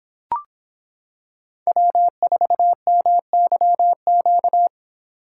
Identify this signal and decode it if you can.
Morse code, heard as a keyed tone, about 26 wpm, 700 Hz. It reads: W4MYQ